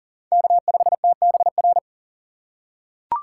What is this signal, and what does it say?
Morse code, heard as a keyed tone, about 40 wpm, 700 Hz. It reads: K5TBR
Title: USA Call Signs - 8x Character Spacing - 40wpm